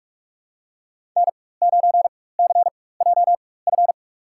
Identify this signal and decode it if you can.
Morse code, heard as a keyed tone, about 46 wpm, spaced as for 21 wpm, 700 Hz. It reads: N9CJF
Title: USA Call Signs - 5 Characters - 4x Character Spacing - 45wpm